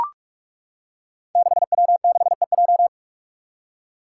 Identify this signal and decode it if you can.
Morse code, heard as a keyed tone, about 42 wpm, 700 Hz. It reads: 6W6EJ
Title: Five Challenging Letter-Number Combinations 45wpm